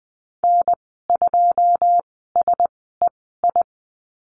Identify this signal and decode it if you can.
Morse code, heard as a keyed tone, about 20 wpm, spaced as for 14 wpm, 700 Hz. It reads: N2SEI
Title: 082 Lesson 14g - Number 2 - Call Signs 20wpm